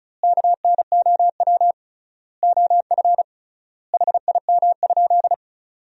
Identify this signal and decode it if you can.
Morse code, heard as a keyed tone, about 36 wpm, 700 Hz. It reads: KNOW OF HIM?
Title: 3x Word-Spacing - Sentences from Top 100 Words - 35wpm